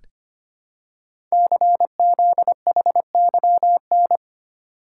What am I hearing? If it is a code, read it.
Morse code, heard as a keyed tone, about 25 wpm, 700 Hz. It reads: CZHYN
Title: Code Groups - 5 Letters - 25wpm